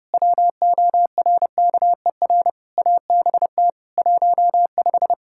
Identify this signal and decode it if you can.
Morse code, heard as a keyed tone, about 30 wpm, 700 Hz. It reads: WORKER ABT 15
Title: K1USN SST QSO Practice 30wpm